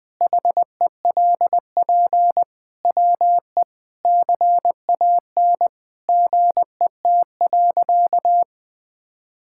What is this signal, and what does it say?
Morse code, heard as a keyed tone, about 21 wpm, 700 Hz. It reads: HELP WE CAN GET.
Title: Sentences with 8-Word Phrases - 20wpm